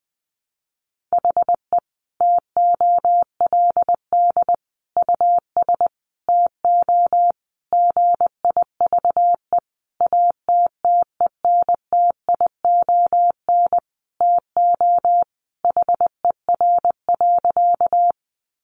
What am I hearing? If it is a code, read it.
Morse code, heard as a keyed tone, about 20 wpm, 700 Hz. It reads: HE TOLD US TO GIVE ATTENTION TO HER.
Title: Sentences from Top 1000 Words 20wpm